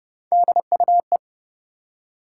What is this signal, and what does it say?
Morse code, heard as a keyed tone, about 30 wpm, 700 Hz. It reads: DUE
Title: Three Letter Words - 30wpm